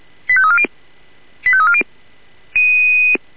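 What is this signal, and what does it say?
Five-tone selective call (ZVEI), heard as quick runs of five tones, tones of about 70 ms. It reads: 85340 85340
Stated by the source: Five-tone ZVEI selective call 85340, received on an SDR.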